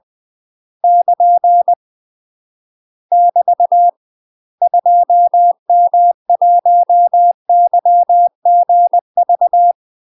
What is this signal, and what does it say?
Morse code, heard as a keyed tone, about 20 wpm, 700 Hz. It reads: <KN> <BT> 2M1YGV
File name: UK MASTER QSOs TRAINING / 20WPM / UK MASTER QSOs 0047